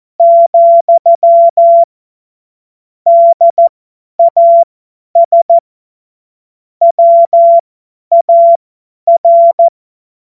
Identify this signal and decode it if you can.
Morse code, heard as a keyed tone, about 14 wpm, spaced as for 10 wpm, 680 Hz. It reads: , DAS WAR
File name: Morsekurs / Witze / Witz 03. Mai 2022 50 F70